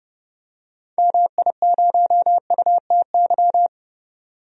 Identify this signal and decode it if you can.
Morse code, heard as a keyed tone, about 30 wpm, 700 Hz. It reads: MI0UTY
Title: DXCC-100 Call Signs - 6 Characters - 30wpm